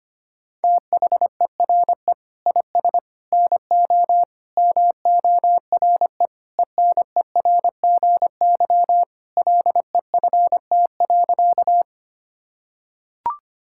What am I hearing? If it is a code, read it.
Morse code, heard as a keyed tone, about 25 wpm, 700 Hz. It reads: THERE IS NO MORE ENERGY LEFT.
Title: Sentences from Top 1000 Words - Review Part 6 - 25wpm